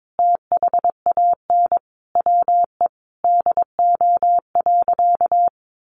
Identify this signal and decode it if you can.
Morse code, heard as a keyed tone, about 22 wpm, 700 Hz. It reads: THAN WE DO.
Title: Sentences from Top 100 Words 22wpm